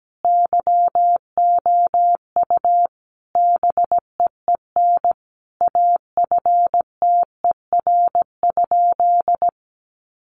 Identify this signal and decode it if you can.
Morse code, heard as a keyed tone, about 17 wpm, 700 Hz. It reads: YOU BEEN AFTER?
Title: Sentences from Top 200 Words 17wpm